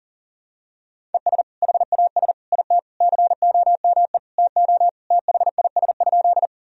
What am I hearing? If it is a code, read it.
Morse code, heard as a keyed tone, about 40 wpm, 700 Hz. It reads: ES HAS IT COME TO THIS?
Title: Sentences from Top 100 Words and Abbreviations 40wpm